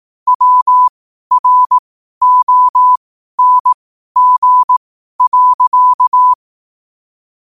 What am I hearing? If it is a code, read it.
Morse code, heard as a keyed tone, about 18 wpm, spaced as for 12 wpm, 1000 Hz. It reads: WRONG.